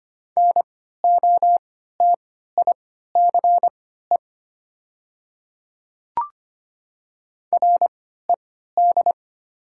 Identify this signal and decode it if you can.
Morse code, heard as a keyed tone, about 25 wpm, spaced as for 14 wpm, 700 Hz. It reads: NOTICE RED
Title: 3x Character Spacing - Top 500 Words - 25wpm